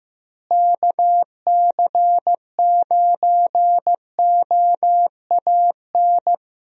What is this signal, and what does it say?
Morse code, heard as a keyed tone, about 14 wpm, 700 Hz. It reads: KC9OAN
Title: USA Call Signs - 6 Characters - Repeated 3x - 15wpm